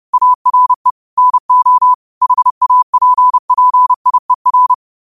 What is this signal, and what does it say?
Morse code, heard as a keyed tone, about 30 wpm, 1000 Hz. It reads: ARE NO HAPPIER
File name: qotdfiles / 30wpm